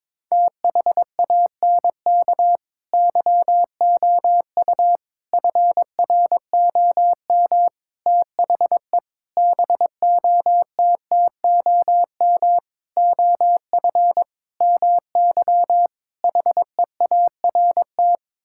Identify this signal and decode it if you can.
Morse code, heard as a keyed tone, about 22 wpm, 700 Hz. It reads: THANK YOU FROM THE BOTTOM OF MY HEART
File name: practice / 22wpm / Sets of 8 Words - 22wpm